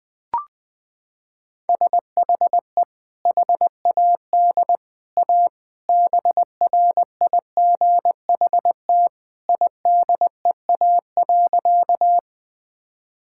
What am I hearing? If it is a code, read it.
Morse code, heard as a keyed tone, about 20 wpm, 700 Hz. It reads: SHE HAD A BRIGHT IDEA.
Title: Sentences with 2-Word Phrases - 20wpm